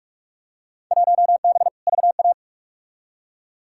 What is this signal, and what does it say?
Morse code, heard as a keyed tone, about 46 wpm, 700 Hz. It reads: 1B VA
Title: Field Day Exchanges - 45wpm